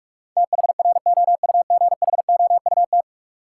Morse code, heard as a keyed tone, about 46 wpm, 700 Hz. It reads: THROUGHOUT
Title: Sentences from Top 800 Words - New Word Drill - 45wpm